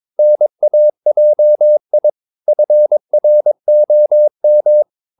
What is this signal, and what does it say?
Morse code, heard as a keyed tone, about 22 wpm, 600 Hz. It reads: NAJI FROM
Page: Romanen Never af Ken Follet på engelsk.